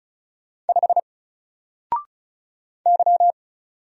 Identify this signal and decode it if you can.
Morse code, heard as a keyed tone, about 33 wpm, 700 Hz. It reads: 5 Y
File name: Single-Letter-Number-35wpm